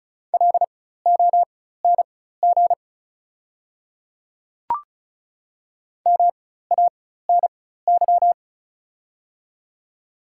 Morse code, heard as a keyed tone, about 36 wpm, spaced as for 16 wpm, 690 Hz. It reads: LONG MANY
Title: 4x Character Spacing - Top 100 Words - 35wpm